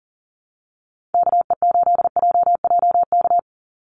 Morse code, heard as a keyed tone, about 39 wpm, 700 Hz. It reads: KE8JJK
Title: USA Call Signs 40wpm